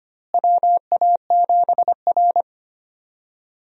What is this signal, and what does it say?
Morse code, heard as a keyed tone, about 25 wpm, 700 Hz. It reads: WA7R
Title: USA Call Signs - 4 Characters - Repeated 2x - 25wpm